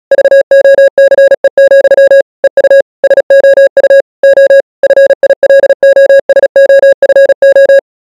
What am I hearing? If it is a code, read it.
Morse code, heard as a keyed tone, about 37 wpm, 550 Hz. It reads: VOCE, EU SOU O FILOSOFO